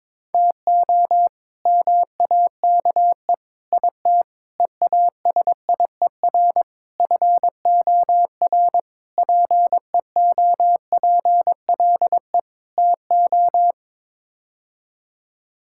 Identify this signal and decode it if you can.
Morse code, heard as a keyed tone, about 22 wpm, 700 Hz. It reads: TO MAKE IT EASIER FOR PEOPLE TO
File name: practice / 22wpm / Sets of 7 Words - 22wpm